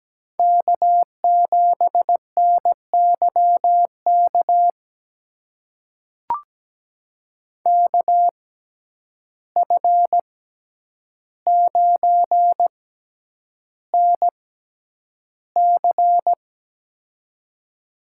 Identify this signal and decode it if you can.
Morse code, heard as a keyed tone, about 17 wpm, spaced as for 9 wpm, 700 Hz. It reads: K7NYK KF9NC
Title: USA Call Signs - 5 Characters - 6x Character Spacing - 17wpm